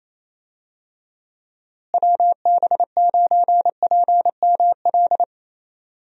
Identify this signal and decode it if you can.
Morse code, heard as a keyed tone, about 28 wpm, 700 Hz. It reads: WB9PML
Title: USA Call Signs - 6 Characters - Repeated 2x - 28wpm